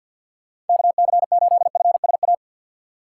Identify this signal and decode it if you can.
Morse code, heard as a keyed tone, about 48 wpm, 700 Hz. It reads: KC8FSA
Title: USA Call Signs - 6 Characters - 6x Character Spacing - 50wpm